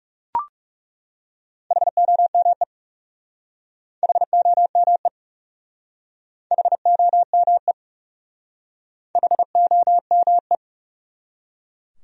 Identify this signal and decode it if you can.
Morse code, heard as a keyed tone, about 38 wpm, 700 Hz. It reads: HOME HOME HOME HOME